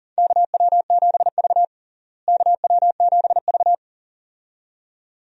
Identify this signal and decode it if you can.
Morse code, heard as a keyed tone, about 40 wpm, 700 Hz. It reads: KW7V KW7V
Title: USA Call Signs - 4 Characters - Repeated 2x - 40wpm